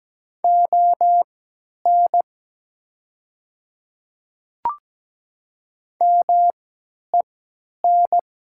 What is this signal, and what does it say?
Morse code, heard as a keyed tone, about 17 wpm, spaced as for 10 wpm, 690 Hz. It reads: ON MEN
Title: 3x Character Spacing - Top 200 Words - 17wpm